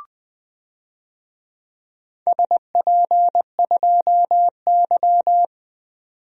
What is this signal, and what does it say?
Morse code, heard as a keyed tone, about 20 wpm, 700 Hz. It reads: SP2Y
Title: DXCC-100 Call Signs - 4 Characters - 20wpm